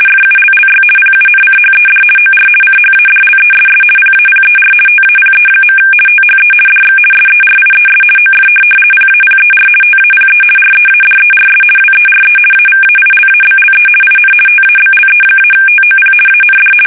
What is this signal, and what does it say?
RTTY, about 50 baud. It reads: FULLY PUNCHED OR CHADLESS PAPER TAPE. THE DX SERIES OF PAPER TAPE READERS WERE AVAILABLE IN TABLE-TOP AND
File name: fsk850-50B / teletype-corp